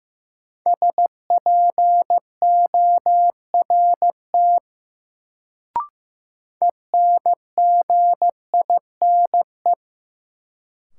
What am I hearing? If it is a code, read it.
Morse code, heard as a keyed tone, about 15 wpm, 700 Hz. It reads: SPORT ENGINE
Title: Sentences from Top 1500 Words - New Word Drill Alt - 15wpm